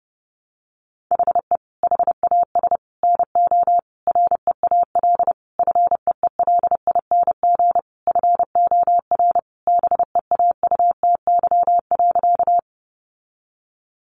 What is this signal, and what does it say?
Morse code, heard as a keyed tone, about 30 wpm, 700 Hz. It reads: HE HAS NO REAL FEELING FOR BEAUTY.
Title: Sentences from Top 1200 Words 30wpm